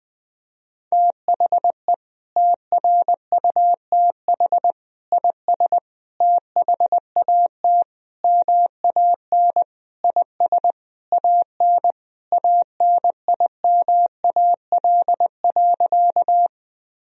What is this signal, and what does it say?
Morse code, heard as a keyed tone, about 20 wpm, 700 Hz. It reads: THE TRUTH IS THAT MAN IS AN ANIMAL.
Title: Sentences from Top 1000 Words - Review Part 1 - 20wpm